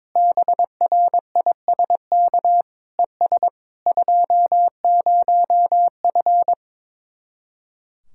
Morse code, heard as a keyed tone, about 22 wpm, 700 Hz. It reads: BRISK ES 20F
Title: Speed-Racing Common QSO Phrases Encore - 22wpm